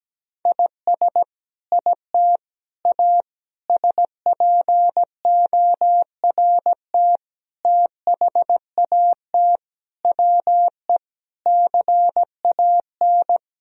Morse code, heard as a keyed tone, about 17 wpm, 700 Hz. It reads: IS IT A SPORT THAT WE CAN
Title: Sentences from Top 1000 Words - Review Part 4 - 17wpm